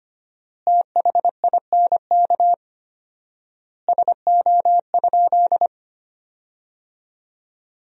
Morse code, heard as a keyed tone, about 25 wpm, 700 Hz. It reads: THINK SO?
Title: 4x Word-Spacing - Sentences from Top 200 Words - 25wpm